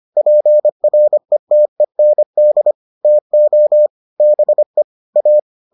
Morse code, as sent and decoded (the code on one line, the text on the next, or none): .--. .-. - . -. -.. / - --- / -... / .-
PRETEND TO BE A